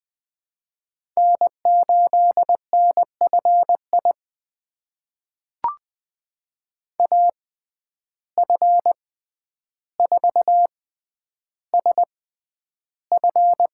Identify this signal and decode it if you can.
Morse code, heard as a keyed tone, about 20 wpm, spaced as for 10 wpm, 700 Hz. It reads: N8NFI AF4SF
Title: USA Call Signs - 5 Characters - 6x Character Spacing - 20wpm